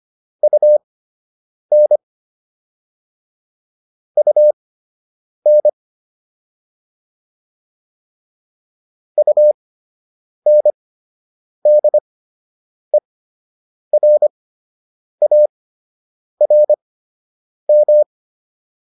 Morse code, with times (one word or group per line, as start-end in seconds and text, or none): 0.4-2.0 UN
4.2-5.7 UN
9.2-18.0 UNDERARM